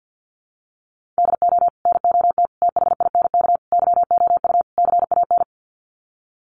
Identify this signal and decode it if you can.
Morse code, heard as a keyed tone, about 50 wpm, 700 Hz. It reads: DO NOT THINK YOU CAN
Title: Sets of 5 Words 50wpm